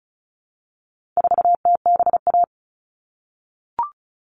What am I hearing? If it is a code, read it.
Morse code, heard as a keyed tone, about 34 wpm, 700 Hz. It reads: 4TBA
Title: All Cut Numbers - 4-Digit 35wpm